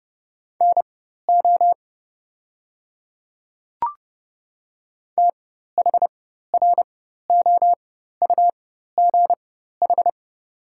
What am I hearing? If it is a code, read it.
Morse code, heard as a keyed tone, about 30 wpm, spaced as for 14 wpm, 700 Hz. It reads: NO THROUGH
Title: 4x Character Spacing - Top 300 Words - 30wpm